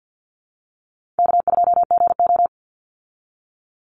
Morse code, heard as a keyed tone, about 50 wpm, 700 Hz. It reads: K2GO
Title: Call Signs - 50wpm